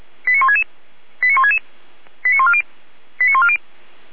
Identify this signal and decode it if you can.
Five-tone selective call (ZVEI), heard as quick runs of five tones, tones of about 70 ms. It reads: 89166 89166 89140 89140